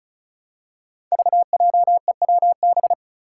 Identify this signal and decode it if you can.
Morse code, heard as a keyed tone, about 34 wpm, 700 Hz. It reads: VJEWB
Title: Five Challenging Letter-Number Combinations 35wpm